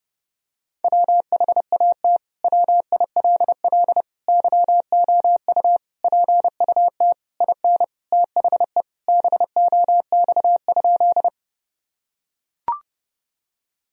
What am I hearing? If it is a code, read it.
Morse code, heard as a keyed tone, about 30 wpm, 700 Hz. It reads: WHAT WILL YOU PUT IN THE BOX?
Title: Sentences from Top 1000 Words - Review Part 2 - 30wpm